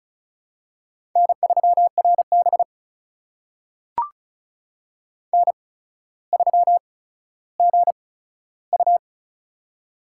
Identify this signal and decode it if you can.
Morse code, heard as a keyed tone, about 36 wpm, spaced as for 15 wpm, 700 Hz. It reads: N3RB N3GU
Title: USA Call Signs - 4 Characters - 8x Character Spacing - 35wpm